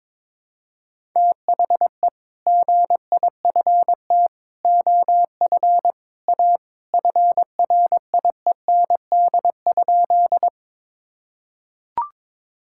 Morse code, heard as a keyed tone, about 22 wpm, 700 Hz. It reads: THE GIFT OF A FRIEND?
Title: Sentences from Top 1000 Words - Review Part 6 - 22wpm